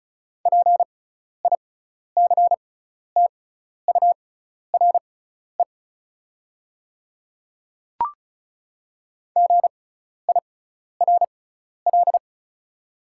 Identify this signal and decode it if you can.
Morse code, heard as a keyed tone, about 36 wpm, spaced as for 12 wpm, 700 Hz. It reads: PICTURE GIRL
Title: 6x Character Spacing - Top 500 Words - 35wpm